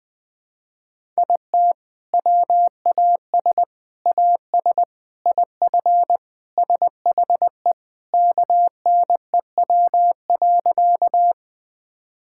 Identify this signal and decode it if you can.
Morse code, heard as a keyed tone, about 20 wpm, 700 Hz. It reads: IT WAS AS IF SHE KNEW.
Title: Sentences with 4-Word Phrases - 20wpm